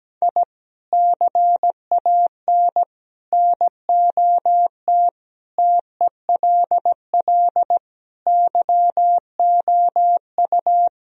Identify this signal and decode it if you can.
Morse code, heard as a keyed tone, about 17 wpm, 700 Hz. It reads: I CAN NOT TELL YOU
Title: Speed-Racing - Sets of 5 Words 17wpm